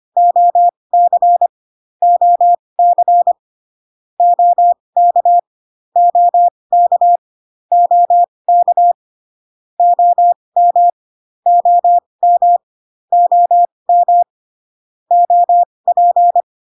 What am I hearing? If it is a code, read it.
Morse code, heard as a keyed tone, about 25 wpm, spaced as for 20 wpm, 700 Hz. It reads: OC OC OK OK OK OM OM OM OP